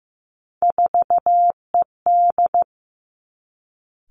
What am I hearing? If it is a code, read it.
Morse code, heard as a keyed tone, about 15 wpm, 700 Hz. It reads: 4ED
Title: Three Challenging Letter-Number Combinations 15wpm